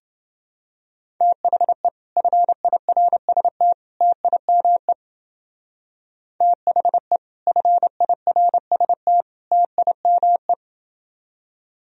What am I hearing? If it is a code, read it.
Morse code, heard as a keyed tone, about 28 wpm, 700 Hz. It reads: THE FIRST TIME THE FIRST TIME